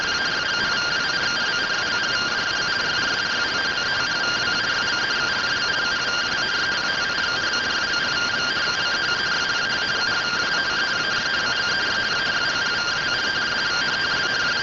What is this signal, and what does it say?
Two-tone teleprinter signal, about 170 Hz shift.
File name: findings / CT AXIAL -485mm 2.gen-12 rtty rx